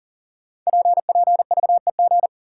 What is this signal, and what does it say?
Morse code, heard as a keyed tone, about 38 wpm, 700 Hz. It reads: PPVEG
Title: Code Groups - 5 Letter:Numbers - 40wpm